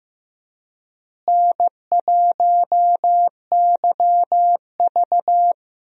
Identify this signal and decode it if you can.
Morse code, heard as a keyed tone, about 15 wpm, 700 Hz. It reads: N1YV
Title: USA Call Signs - 4 Characters - Repeated 2x - 15wpm